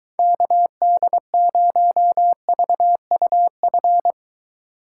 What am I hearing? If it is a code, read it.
Morse code, heard as a keyed tone, about 23 wpm, 700 Hz. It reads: KD0VUF